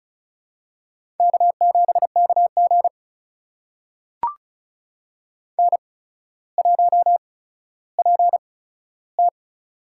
Morse code, heard as a keyed tone, about 36 wpm, spaced as for 15 wpm, 700 Hz. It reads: K7KG N1PT
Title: USA Call Signs - 4 Characters - 8x Character Spacing - 35wpm